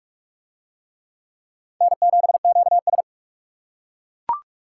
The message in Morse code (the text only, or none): N7QS